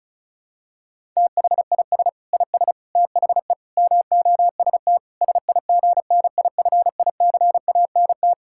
Morse code, heard as a keyed tone, about 36 wpm, 700 Hz. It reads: THIS IS THE MOST SIGNIFICANT
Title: Sentences from Top 1000 Words - Review Part 6 - 35wpm